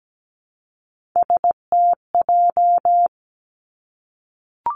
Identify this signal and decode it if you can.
Morse code, heard as a keyed tone, about 17 wpm, 700 Hz. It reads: STJ
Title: Three Challenging Letter-Number Combinations 17wpm